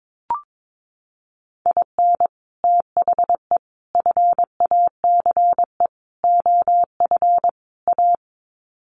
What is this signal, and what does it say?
Morse code, heard as a keyed tone, about 22 wpm, 700 Hz. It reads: IN THE FACE OF A
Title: Sets of 5 Words Encore 22wpm